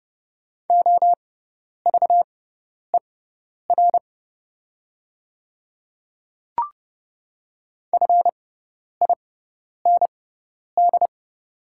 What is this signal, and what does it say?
Morse code, heard as a keyed tone, about 30 wpm, spaced as for 10 wpm, 700 Hz. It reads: OVER FIND
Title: 6x Character Spacing - Top 100 Words - 30wpm